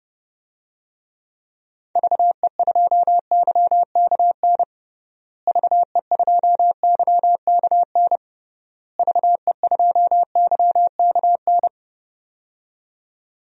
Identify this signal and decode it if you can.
Morse code, heard as a keyed tone, about 32 wpm, 700 Hz. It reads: VE2YKN VE2YKN VE2YKN
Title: DXCC-100 Call Signs - 6 Characters - Repeated 3x - 30wpm